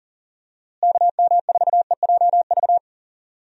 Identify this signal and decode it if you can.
Morse code, heard as a keyed tone, about 37 wpm, 700 Hz. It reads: KM4EJV
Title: USA Call Signs - 6 Characters - Repeated 2x - 40wpm